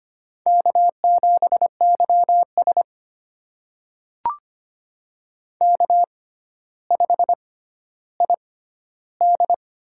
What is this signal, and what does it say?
Morse code, heard as a keyed tone, about 25 wpm, spaced as for 13 wpm, 700 Hz. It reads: K7YS K5ID
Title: USA Call Signs - 4 Characters - 6x Character Spacing - 25wpm